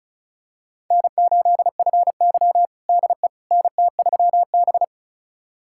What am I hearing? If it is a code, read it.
Morse code, heard as a keyed tone, about 34 wpm, 700 Hz. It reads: N8FY DE NT3B
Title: USA Callsign DE Callsign - 4 Characters - 35wpm